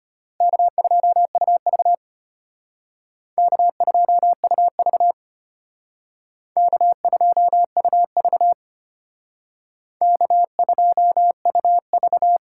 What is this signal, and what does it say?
Morse code, heard as a keyed tone, about 32 wpm, 700 Hz. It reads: K2UV K2UV K2UV K2UV